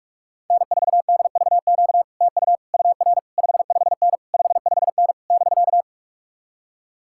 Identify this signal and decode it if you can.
Morse code, heard as a keyed tone, about 61 wpm, spaced as for 53 wpm, 700 Hz. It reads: N4DVQ TU UR 55N 55N <BK>
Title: Speed-Racing Common QSO Phrases - 45wpm